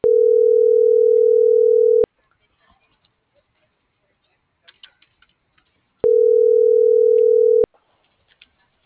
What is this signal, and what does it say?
Ringback tone (North America), 440 + 480 Hz.